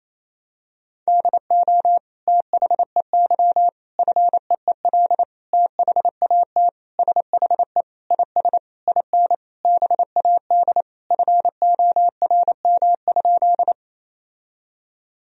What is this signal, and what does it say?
Morse code, heard as a keyed tone, about 28 wpm, 700 Hz. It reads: DO THEY FEEL THAT SHE IS IN BAD FORM?
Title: Sentences from Top 1000 Words - Review Part 1 - 28wpm